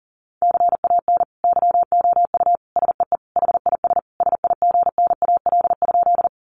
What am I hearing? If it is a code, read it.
Morse code, heard as a keyed tone, about 40 wpm, 700 Hz. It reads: CAN YOU SEE HIS SIGNAL?
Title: Sentences from Top 1400 Words 40wpm